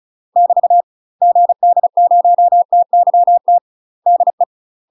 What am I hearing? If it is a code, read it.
Morse code, heard as a keyed tone, about 34 wpm, 700 Hz. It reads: <BT> GD0TYT DE
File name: UK MASTER QSOs TRAINING/35WPM